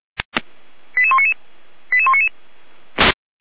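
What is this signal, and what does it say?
Five-tone selective call (ZVEI), heard as quick runs of five tones, tones of about 70 ms. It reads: 88199 88199
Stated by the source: ZVEI five-tone selective call received on an SDR.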